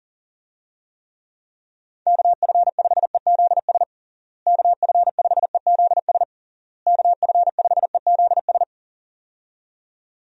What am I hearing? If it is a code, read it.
Morse code, heard as a keyed tone, about 40 wpm, 700 Hz. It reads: KF5EZS KF5EZS KF5EZS
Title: USA Call Signs - 6 Characters - Repeated 3x - 40wpm